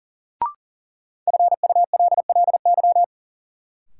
Morse code, heard as a keyed tone, about 40 wpm, 700 Hz. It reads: FULLY
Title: Sentences from Top 1300 Words - New Word Drill Alt - 40wpm